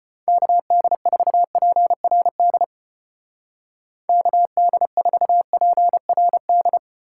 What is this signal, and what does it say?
Morse code, heard as a keyed tone, about 29 wpm, 700 Hz. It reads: KD4PRD KD4PRD
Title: Speed Racing - USA 2x3 Call Signs 25wpm